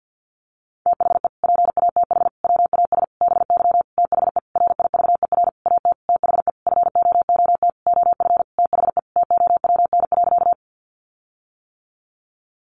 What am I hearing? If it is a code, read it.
Morse code, heard as a keyed tone, about 50 wpm, 700 Hz. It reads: THE PATH WAS BY THE RIVER AT THE FOOT OF THE TOWN.
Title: Sentences from Top 1400 Words 50wpm